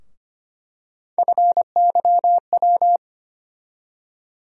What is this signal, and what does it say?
Morse code, heard as a keyed tone, about 25 wpm, 700 Hz. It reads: FYW
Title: USA Call Sign Postfixes - 3 Letters 25wpm